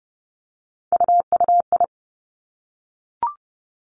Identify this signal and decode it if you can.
Morse code, heard as a keyed tone, about 30 wpm, 700 Hz. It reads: UUI
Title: Three Challenging Letter-Number Combinations 30wpm